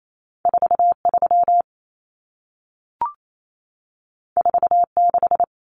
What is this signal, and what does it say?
Morse code, heard as a keyed tone, about 28 wpm, 700 Hz. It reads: 43 46
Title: Common Cut Numbers - 2-Digit 28wpm